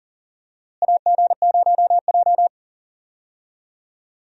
Morse code, heard as a keyed tone, about 40 wpm, 700 Hz. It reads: AG0J
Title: USA Call Signs - 4 Characters - Repeated 2x - 40wpm